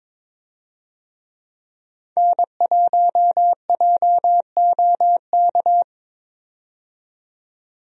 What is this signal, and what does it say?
Morse code, heard as a keyed tone, about 22 wpm, 700 Hz. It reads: N1JOK
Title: USA Call Signs - 5 Characters - 22wpm